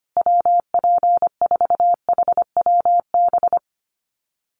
Morse code, heard as a keyed tone, about 25 wpm, 700 Hz. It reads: WP4HWB
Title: USA Call Signs Repeated 2x 25wpm